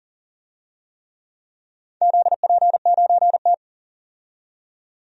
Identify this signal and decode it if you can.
Morse code, heard as a keyed tone, about 37 wpm, 700 Hz. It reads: ZP9T
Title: DXCC-100 Call Signs - 4 Characters - 40wpm